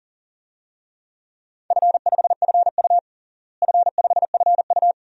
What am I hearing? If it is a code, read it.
Morse code, heard as a keyed tone, about 40 wpm, 700 Hz. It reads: F5FU F5FU
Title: DXCC-100 Call Signs - 4 Characters - Repeated 2x - 40wpm